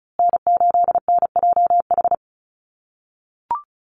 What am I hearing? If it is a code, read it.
Morse code, heard as a keyed tone, about 34 wpm, 700 Hz. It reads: N8NJH
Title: USA Call Signs Repeated 3x 35wpm